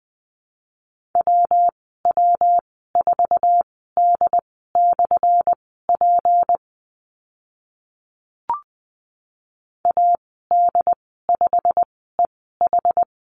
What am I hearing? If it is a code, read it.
Morse code, heard as a keyed tone, about 20 wpm, spaced as for 15 wpm, 700 Hz. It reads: WW4D/P AD5EH
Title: 179 Lesson 29g - Slash - Call Signs 20wpm